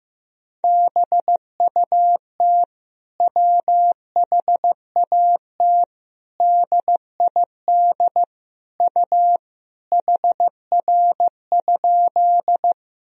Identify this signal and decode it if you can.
Morse code, heard as a keyed tone, about 15 wpm, 700 Hz. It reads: BUT WHAT DID U HR?
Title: Sentences from Top 300 Words and Abbreviations 15wpm